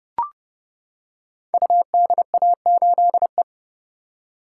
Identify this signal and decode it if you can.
Morse code, heard as a keyed tone, about 30 wpm, 700 Hz. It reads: UDA8E
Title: Code Groups - 5 Letter:Numbers - 30wpm